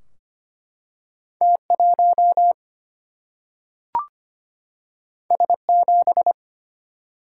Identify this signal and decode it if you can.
Morse code, heard as a keyed tone, about 25 wpm, 700 Hz. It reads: T1 S7
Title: Single-Letter Single-Number 25wpm